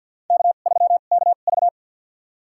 Morse code, heard as a keyed tone, about 47 wpm, spaced as for 35 wpm, 690 Hz. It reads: K3KV
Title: USA Call Signs - 2x Character Spacing - 50wpm